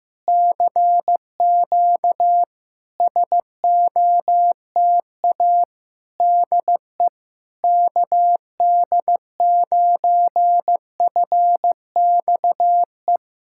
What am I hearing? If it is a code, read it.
Morse code, heard as a keyed tone, about 15 wpm, 700 Hz. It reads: CQ SOTA DE KD9FXE
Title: SOTA QSO Practice 15wpm